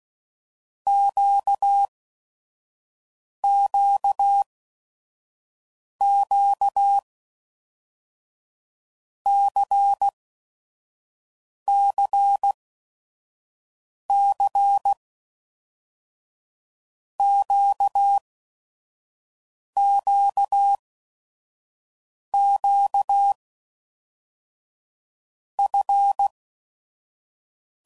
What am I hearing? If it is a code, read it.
Morse code, heard as a keyed tone, about 16 wpm, 800 Hz. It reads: Q Q Q C C C Q Q Q F
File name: Basic Letter Groups By7Groups / LFCYQ / LFCYQ@5 f15